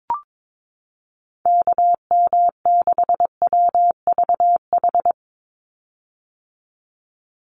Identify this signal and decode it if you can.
Morse code, heard as a keyed tone, about 22 wpm, 700 Hz. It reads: KM6WVH
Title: USA Call Signs 22wpm